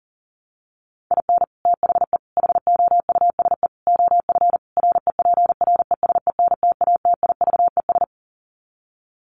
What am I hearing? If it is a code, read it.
Morse code, heard as a keyed tone, about 40 wpm, 700 Hz. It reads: IN THE HOUSE OF REPRESENTATIVES
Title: Sets of 5 Words Encore 40wpm